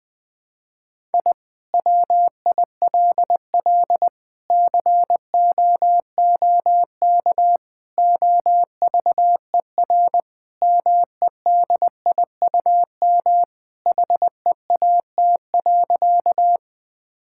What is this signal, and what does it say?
Morse code, heard as a keyed tone, about 20 wpm, 700 Hz. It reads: I WILL COOK OVER MEDIUM HEAT.
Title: Sentences from Top 1000 Words - Review Part 6 - 20wpm